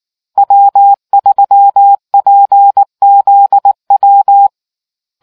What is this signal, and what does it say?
Morse code, heard as a keyed tone, about 19 wpm, 800 Hz. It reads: W3PZW